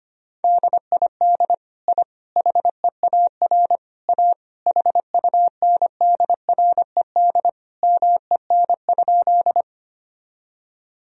Morse code, heard as a keyed tone, about 25 wpm, 700 Hz. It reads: DID I HEAR A HUNDRED MEN?
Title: Sentences from Top 1000 Words - Review Part 2 - 25wpm